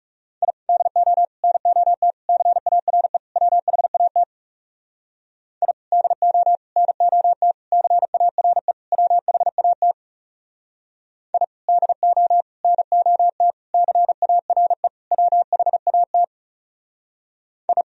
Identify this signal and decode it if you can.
Morse code, heard as a keyed tone, about 37 wpm, 700 Hz. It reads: I DO NOT CARE WHAT I DO NOT CARE WHAT I DO NOT CARE WHAT I
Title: Speed-Racing - Sets of 5 Words 30wpm